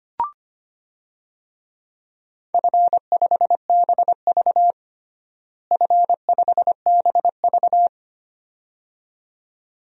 Morse code, heard as a keyed tone, about 25 wpm, 700 Hz. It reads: F5BV F5BV
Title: DXCC-100 Call Signs - 4 Characters - Repeated 2x - 25wpm